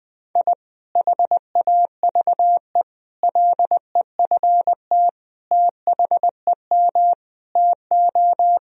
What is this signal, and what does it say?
Morse code, heard as a keyed tone, about 20 wpm, 700 Hz. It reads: I HAVE LEFT THEM TO
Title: Sentences from Top 1000 Words - Review Part 2 - 20wpm